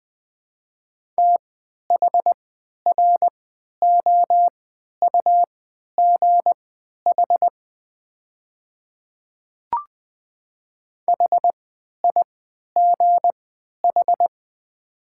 Morse code, heard as a keyed tone, about 20 wpm, spaced as for 11 wpm, 690 Hz. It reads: THROUGH HIGH
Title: 3x Character Spacing - Top 200 Words - 20wpm